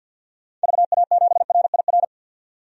Morse code, heard as a keyed tone, about 52 wpm, 700 Hz. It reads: VA7RIR
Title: DXCC-100 Call Signs - 6 Characters - 50wpm